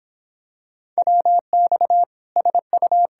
Morse code, heard as a keyed tone, about 26 wpm, 700 Hz. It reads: WX SU